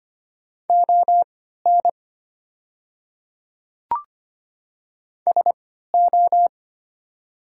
Morse code, heard as a keyed tone, about 25 wpm, spaced as for 14 wpm, 700 Hz. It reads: ON SO
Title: 3x Character Spacing - Top 300 Words - 25wpm